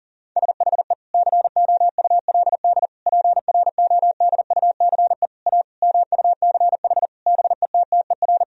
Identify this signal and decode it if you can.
Morse code, heard as a keyed tone, about 40 wpm, 700 Hz. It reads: SHE COULD PRODUCE A MUCH BETTER
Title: Sentences from Top 1000 Words - Review Part 3 - 40wpm